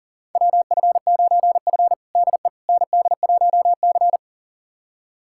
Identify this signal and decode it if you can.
Morse code, heard as a keyed tone, about 40 wpm, 700 Hz. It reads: WF9F DE ND1C
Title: USA Callsign DE Callsign - 4 Characters - 40wpm